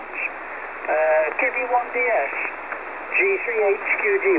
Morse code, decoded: none